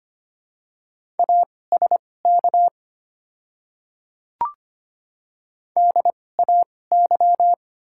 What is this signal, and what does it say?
Morse code, heard as a keyed tone, about 25 wpm, spaced as for 18 wpm, 690 Hz. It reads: ASK DAY
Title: 2x Character Spacing - Top 300 Words - 25wpm